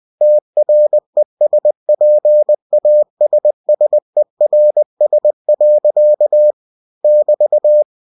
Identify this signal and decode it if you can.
Morse code, heard as a keyed tone, about 20 wpm, 600 Hz. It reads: TRESPASSERS. <BT>
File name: cw-books / winnie-the-pooh / winnie-20wpm-page-0020